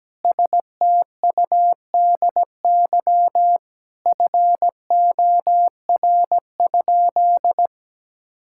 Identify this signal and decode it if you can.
Morse code, heard as a keyed tone, about 17 wpm, 700 Hz. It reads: STUDY FOR?
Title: Sentences from Top 1000 Words - Review Part 6 - 17wpm